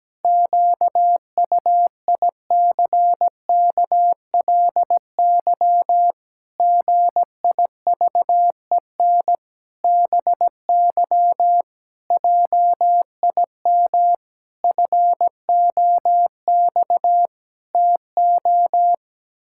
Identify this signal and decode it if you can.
Morse code, heard as a keyed tone, about 17 wpm, 700 Hz. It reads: QUICKLY GIVEN BY JIM FOX TO
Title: Pangram Challenge v2 - 17wpm